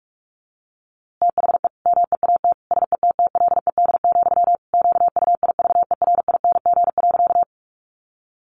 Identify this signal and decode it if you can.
Morse code, heard as a keyed tone, about 44 wpm, 700 Hz. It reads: THE MEAT SETTLED, QUIVERING.
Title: A Sound Of Thunder 45wpm